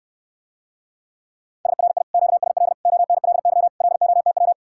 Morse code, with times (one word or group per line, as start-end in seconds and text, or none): 1.7-2.0 SRI
2.1-2.7 QSL
2.9-3.7 ONLY
3.8-4.5 LOTW